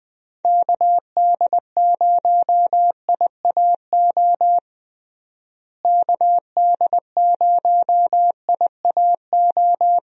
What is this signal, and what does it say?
Morse code, heard as a keyed tone, about 20 wpm, 700 Hz. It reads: KD0IAO KD0IAO
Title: USA Call Signs - 6 Characters - Repeated 3x - 20wpm